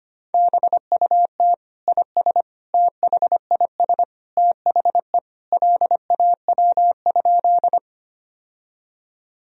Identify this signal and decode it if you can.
Morse code, heard as a keyed tone, about 25 wpm, 700 Hz. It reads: BUT IS THIS THE LAW?
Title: Sentences from Top 1000 Words - Review Part 3 - 25wpm